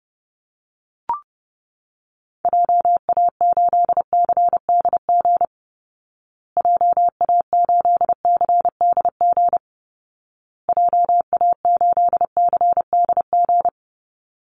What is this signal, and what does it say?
Morse code, heard as a keyed tone, about 28 wpm, 700 Hz. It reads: JA8CDG JA8CDG JA8CDG
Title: Call Signs Repeated 2x - 30wpm